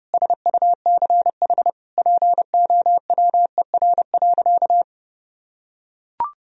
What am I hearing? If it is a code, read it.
Morse code, heard as a keyed tone, about 32 wpm, 700 Hz. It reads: SUCH POWER.
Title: Sentences from Top 1000 Words - Review Part 2 - 30wpm